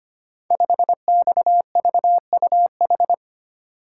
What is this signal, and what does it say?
Morse code, heard as a keyed tone, about 25 wpm, 700 Hz. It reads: 5XVUH
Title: Code Groups - 5 Letter:Numbers - 25wpm